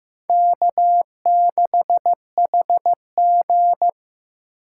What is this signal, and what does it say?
Morse code, heard as a keyed tone, about 18 wpm, 700 Hz. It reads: K6HG
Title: Speed Racing - USA 1x2 Call Signs 15wpm